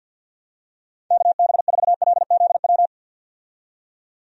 Morse code, heard as a keyed tone, about 51 wpm, 700 Hz. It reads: KB4LZW
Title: USA Call Signs - 6 Characters - 50wpm